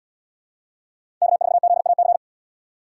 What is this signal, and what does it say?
Morse code, heard as a keyed tone, about 77 wpm, 700 Hz. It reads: K4ZAC